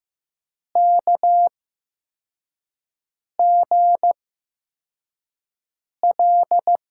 Morse code, heard as a keyed tone, about 15 wpm, spaced as for 4 wpm, 700 Hz. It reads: KGL
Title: USA Call Signs - 5 Characters - 8x Character Spacing - 15wpm